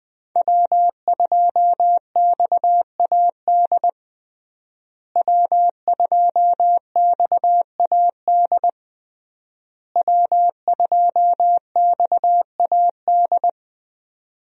Morse code, heard as a keyed tone, about 20 wpm, 700 Hz. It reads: W2XAD W2XAD W2XAD
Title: USA Call Signs - 5 Characters - Repeated 3x - 20wpm